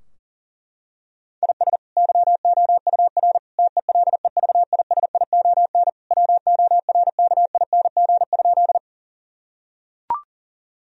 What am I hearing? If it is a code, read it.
Morse code, heard as a keyed tone, about 40 wpm, 700 Hz. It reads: IS YOUR TELEVISION WORKING?
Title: Sentences from Top 1000 Words - Review Part 4 - 40wpm